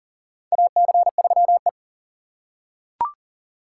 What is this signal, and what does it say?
Morse code, heard as a keyed tone, about 40 wpm, 700 Hz. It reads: AC3E
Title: USA Call Signs - 4 Characters - 3x Character Spacing - 40wpm